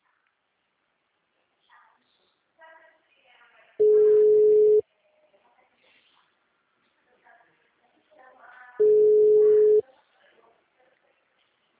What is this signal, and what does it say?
Ringback tone (Europe, 425 Hz), with speech.
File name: out- -2014- - - .15841